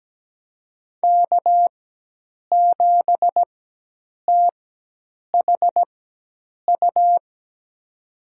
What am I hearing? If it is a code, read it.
Morse code, heard as a keyed tone, about 17 wpm, spaced as for 8 wpm, 700 Hz. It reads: K7THU
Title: USA Call Signs - 5 Characters - 4x Character Spacing - 17wpm